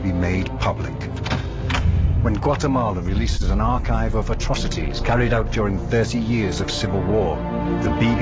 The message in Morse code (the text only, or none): none